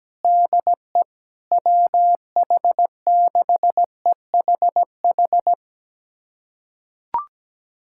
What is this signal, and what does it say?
Morse code, heard as a keyed tone, about 17 wpm, 700 Hz. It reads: DE WH6EHH
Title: USA Callsign DE Callsign - 6 Characters - 17wpm